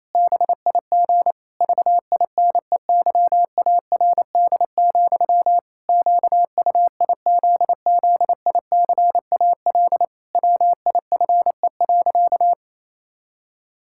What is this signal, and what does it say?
Morse code, heard as a keyed tone, about 28 wpm, 700 Hz. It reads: BIG VINEYARD, QUIZZICAL WIFE.
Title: Pangram Challenge v2 - 28wpm